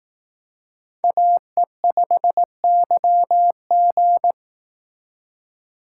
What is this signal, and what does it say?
Morse code, heard as a keyed tone, about 18 wpm, 700 Hz. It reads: AE5YG
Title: Speed Racing - USA 2x2 Call Signs 15wpm